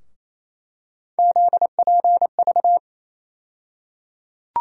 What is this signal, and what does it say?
Morse code, heard as a keyed tone, about 28 wpm, 700 Hz. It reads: ZPV
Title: USA Call Sign Postfixes - 3 Letters 28wpm